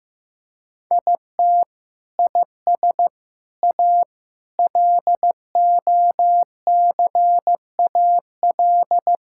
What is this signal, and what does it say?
Morse code, heard as a keyed tone, about 15 wpm, 700 Hz. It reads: IT IS A LOCAL
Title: Sentences from Top 1000 Words - Review Part 5 - 15wpm